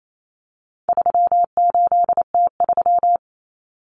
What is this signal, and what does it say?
Morse code, heard as a keyed tone, about 28 wpm, 700 Hz. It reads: 38T3
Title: Common Cut Numbers - 4-Digit 28wpm